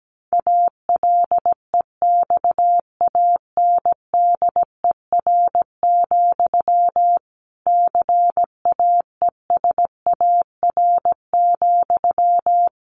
Morse code, heard as a keyed tone, about 17 wpm, 700 Hz. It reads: ALEXANDER, CAESAR,
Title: A Sound Of Thunder 17wpm